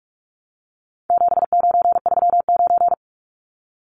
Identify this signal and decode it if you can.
Morse code, heard as a keyed tone, about 44 wpm, 700 Hz. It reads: 7939
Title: Numbers - 4-Digit 45wpm